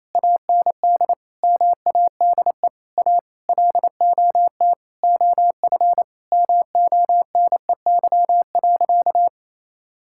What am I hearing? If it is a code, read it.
Morse code, heard as a keyed tone, about 28 wpm, 700 Hz. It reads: AND MADE A LOT OF MONEY.
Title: Sentences from Top 1000 Words - Review Part 6 - 28wpm